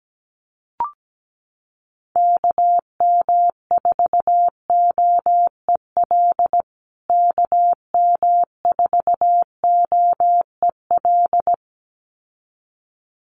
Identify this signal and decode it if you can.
Morse code, heard as a keyed tone, about 17 wpm, 700 Hz. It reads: KM4OEL KM4OEL
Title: USA Call Signs Repeated 2x 17wpm